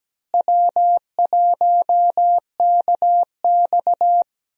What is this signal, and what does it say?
Morse code, heard as a keyed tone, about 17 wpm, 700 Hz. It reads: W1KX
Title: USA Call Signs - 4 Characters - Repeated 2x - 17wpm